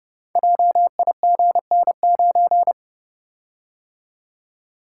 Morse code, heard as a keyed tone, about 29 wpm, 700 Hz. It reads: JIGN9
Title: Code Groups - 5 Letter:Numbers - 30wpm